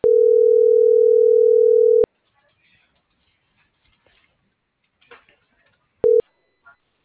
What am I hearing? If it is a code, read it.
Ringback tone (North America), 440 + 480 Hz.